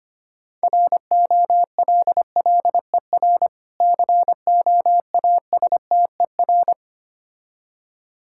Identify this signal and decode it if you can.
Morse code, heard as a keyed tone, about 25 wpm, 700 Hz. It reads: ROLLER COASTER
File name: Easy Words - Speed-Racing with Hints - 25wpm